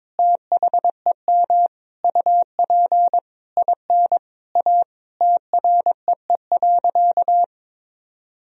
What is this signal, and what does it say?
Morse code, heard as a keyed tone, about 22 wpm, 700 Hz. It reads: THEM UP IN A TREE.
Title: Sentences from Top 1000 Words - Review Part 1 - 22wpm